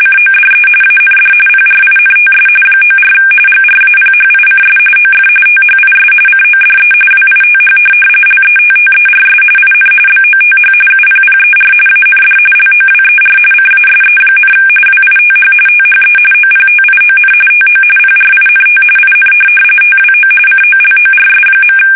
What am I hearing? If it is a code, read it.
RTTY, about 45.45 baud. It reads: CODE AND PRINTS 94 GRAPHICS OF THIS CODE WITH LETTERS IN BOTH UPPER AND LOWER CASE, WITH THE OPTION TO PRINT IN TWO COLORS.
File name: fsk850 / teletype-corp